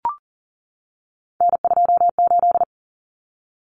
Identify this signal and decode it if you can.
Morse code, heard as a keyed tone, about 40 wpm, 700 Hz. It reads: N28
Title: Common Cut Numbers - 3-Digit 40wpm